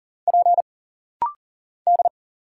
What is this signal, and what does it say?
Morse code, heard as a keyed tone, about 40 wpm, 700 Hz. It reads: P D
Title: Single Letters - Rapid-Fire 40wpm